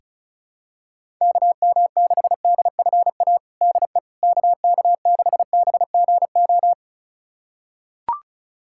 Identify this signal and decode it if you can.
Morse code, heard as a keyed tone, about 34 wpm, 700 Hz. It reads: KM6DFA DE KK6BGO
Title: USA Callsign DE Callsign - 6 Characters - 35wpm